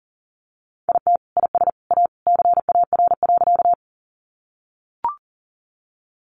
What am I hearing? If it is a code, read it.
Morse code, heard as a keyed tone, about 37 wpm, 700 Hz. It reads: IT IS A CAR.
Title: Easy Phrases 40wpm